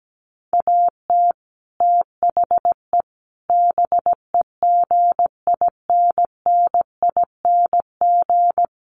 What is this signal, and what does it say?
Morse code, heard as a keyed tone, about 17 wpm, 700 Hz. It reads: AT THE BEGINNING
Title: Sets of 4 Words 17wpm